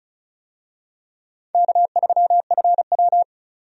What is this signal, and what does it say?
Morse code, heard as a keyed tone, about 36 wpm, 700 Hz. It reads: K3FW
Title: USA Call Signs - 4 Characters - 35wpm